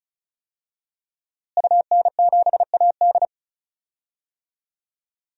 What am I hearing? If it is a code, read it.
Morse code, heard as a keyed tone, about 36 wpm, 700 Hz. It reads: UN7AD
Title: DXCC-100 Call Signs - 5 Characters - Repeated 3x - 35wpm